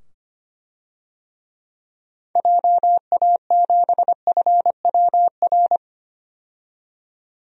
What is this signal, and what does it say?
Morse code, heard as a keyed tone, about 25 wpm, 700 Hz. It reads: JA7FWR
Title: DXCC-100 Call Signs - 6 Characters - 25wpm